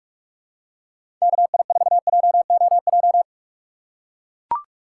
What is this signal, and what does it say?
Morse code, heard as a keyed tone, about 44 wpm, 700 Hz. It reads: KI4JOJ
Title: USA Call Signs - 6 Characters - 6x Character Spacing - 45wpm